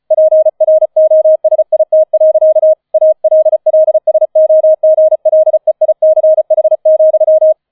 Morse code, heard as a keyed tone, about 34 wpm, 630 Hz. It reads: PROSIT. ALLSOGLEICH,
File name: 005 Fünfter Streich 35wpm